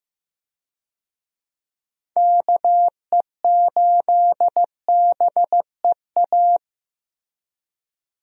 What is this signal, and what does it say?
Morse code, heard as a keyed tone, about 15 wpm, 700 Hz. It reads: KE8BEA
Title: USA Call Signs - 6 Characters - 15wpm